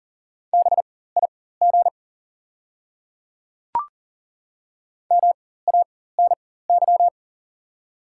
Morse code, heard as a keyed tone, about 40 wpm, spaced as for 19 wpm, 690 Hz. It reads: BIG MANY
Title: 4x Character Spacing - Top 200 Words - 40wpm